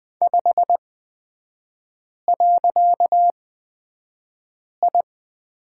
Morse code, heard as a keyed tone, about 20 wpm, spaced as for 5 wpm, 700 Hz. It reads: 5.I
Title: 255 Lesson 41e - Period - Three Characters 20wpm